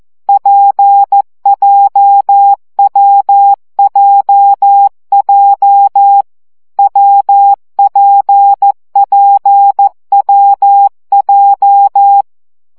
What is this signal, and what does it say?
Morse code, heard as a keyed tone, about 14 wpm, 800 Hz. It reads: PJWJJ WPPWJ
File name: CW gyakorlas w p j 12 baud